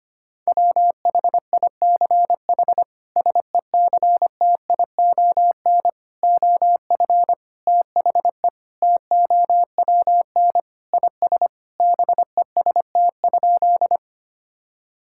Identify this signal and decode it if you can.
Morse code, heard as a keyed tone, about 25 wpm, 700 Hz. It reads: WHICH SECTION OF THE TOWN IS BEST?
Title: Sentences from Top 1000 Words - Review Part 5 - 25wpm